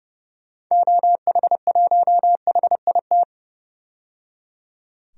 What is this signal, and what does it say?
Morse code, heard as a keyed tone, about 30 wpm, 700 Hz. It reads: OH1HIT
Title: DXCC-100 Call Signs - 6 Characters - 30wpm